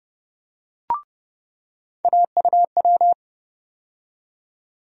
Morse code, heard as a keyed tone, about 31 wpm, 700 Hz. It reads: AUW
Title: USA Call Sign Postfixes - 3 Letters 30wpm